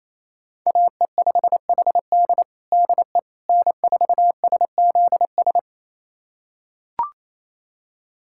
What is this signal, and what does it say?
Morse code, heard as a keyed tone, about 28 wpm, 690 Hz. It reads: AE5HD DE N4SZS
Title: USA Callsign DE Callsign - 5 Characters - 28wpm